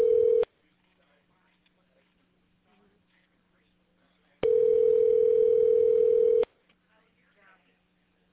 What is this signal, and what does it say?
Ringback tone (North America), 440 + 480 Hz.